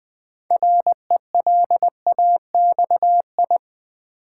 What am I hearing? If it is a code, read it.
Morse code, heard as a keyed tone, about 20 wpm, 700 Hz. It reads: RELAXI
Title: Build A Word - 2 Letters At A Time - 8-Letter Words - 20wpm